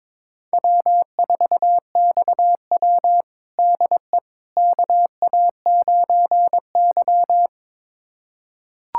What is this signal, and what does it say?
Morse code, heard as a keyed tone, about 21 wpm, 700 Hz. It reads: W4XW DE KA9Y
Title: USA Callsign DE Callsign - 4 Characters - 22wpm